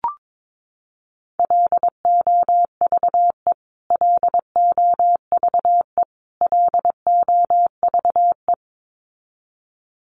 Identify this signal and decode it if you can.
Morse code, heard as a keyed tone, about 22 wpm, 700 Hz. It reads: LOVE LOVE LOVE
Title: Top 500 Words Repeated 22wpm